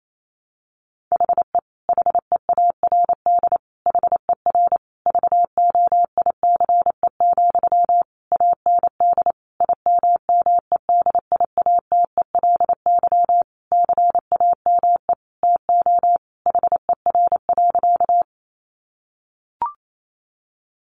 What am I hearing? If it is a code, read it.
Morse code, heard as a keyed tone, about 28 wpm, 700 Hz. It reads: HE HEARD HER VOICE, AND IMMEDIATELY CAME TO HER.
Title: Sentences from Top 1100 Words 28wpm